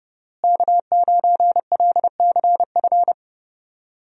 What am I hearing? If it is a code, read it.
Morse code, heard as a keyed tone, about 29 wpm, 700 Hz. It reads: K9LCF
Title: USA Call Signs - 5 Characters - Repeated 2x - 30wpm